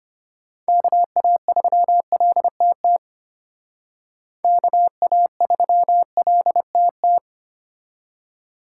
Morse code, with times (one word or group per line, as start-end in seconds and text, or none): 0.7-3.0 KA3LTT
4.4-7.2 KA3LTT